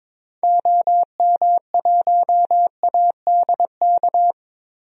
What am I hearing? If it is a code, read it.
Morse code, heard as a keyed tone, about 22 wpm, 700 Hz. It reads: OM1ADK
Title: DXCC-100 Call Signs - 6 Characters - 22wpm